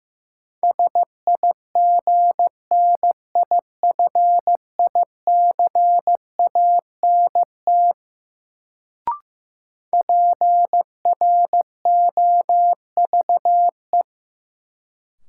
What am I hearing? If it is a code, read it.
Morse code, heard as a keyed tone, about 16 wpm, 700 Hz. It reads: SIGNIFICANT PROVE
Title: Sentences from Top 1600 Words - New Word Drill Alt - 15wpm